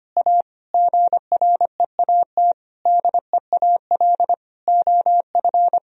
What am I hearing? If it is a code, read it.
Morse code, heard as a keyed tone, about 25 wpm, 700 Hz. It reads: A GREAT DEAL OF
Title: Sentences from Top 1000 Words - Review Part 1 - 25wpm